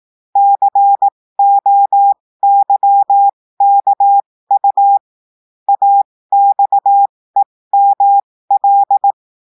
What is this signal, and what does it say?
Morse code, heard as a keyed tone, about 18 wpm, spaced as for 15 wpm, 800 Hz. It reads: COYKU AXEML
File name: cours-du-lundi files / cours-1580-texte-chiffre-15-mpm